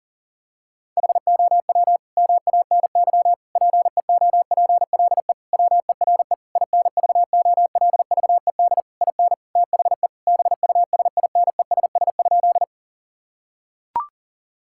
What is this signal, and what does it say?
Morse code, heard as a keyed tone, about 40 wpm, 700 Hz. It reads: HOW MANY PEOPLE WERE INVOLVED IN THE BUSINESS?
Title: Sentences from Top 1000 Words - Review Part 5 - 40wpm